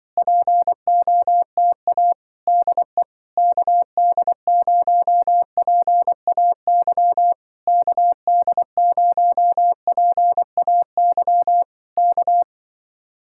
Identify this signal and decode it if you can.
Morse code, heard as a keyed tone, about 24 wpm, 700 Hz. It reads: POTA DE KD0PAY KD0PAY K
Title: Speed-Racing Common QSO Phrases - 20wpm